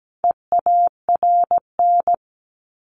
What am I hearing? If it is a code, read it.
Morse code, heard as a keyed tone, about 17 wpm, 700 Hz. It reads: EARN
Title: Top 1251-1500 Words 17wpm